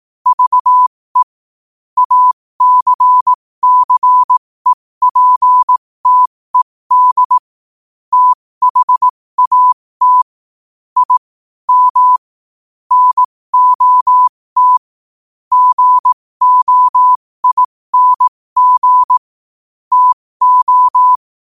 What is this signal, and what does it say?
Morse code, heard as a keyed tone, about 18 wpm, spaced as for 15 wpm, 1000 Hz. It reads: VE ACCEPTED THAT I M NOT GOING TO
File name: qotdfiles / 15wpm